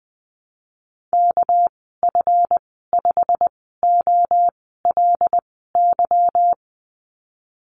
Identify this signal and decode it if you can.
Morse code, heard as a keyed tone, about 19 wpm, spaced as for 14 wpm, 700 Hz. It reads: KF5OLY
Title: 213 Lesson 34g - Number 8 - Call Signs 20wpm